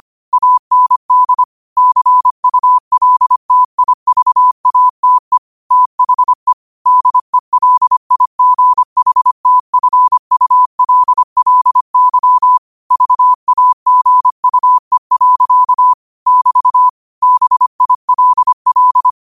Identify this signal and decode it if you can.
Morse code, heard as a keyed tone, about 25 wpm, 1000 Hz. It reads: AND CULTIVATE THE DELIGHTFULLY VAGUE. <BT> BILL